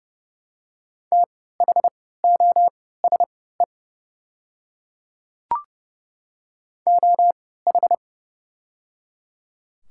Morse code, heard as a keyed tone, about 30 wpm, spaced as for 17 wpm, 700 Hz. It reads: THOSE OH
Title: 3x Character Spacing - Top 500 Words - 30wpm